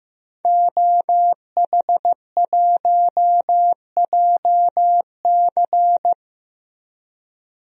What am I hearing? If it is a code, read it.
Morse code, heard as a keyed tone, about 15 wpm, 700 Hz. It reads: OH1JC
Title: DXCC-100 Call Signs - 5 Characters - 15wpm